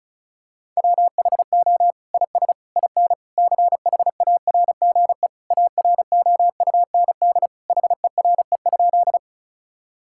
Morse code, heard as a keyed tone, about 34 wpm, 700 Hz. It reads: WHO IS IN CHARGE AROUND HERE?
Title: Sentences from Top 1000 Words - Review Part 6 - 35wpm